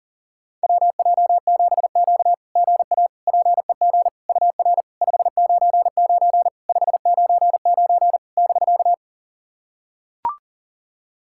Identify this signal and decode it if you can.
Morse code, heard as a keyed tone, about 40 wpm, 690 Hz. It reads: WJ7Q GA PEG UR 599 599 <BK>
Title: SOTA QSO Practice 40wpm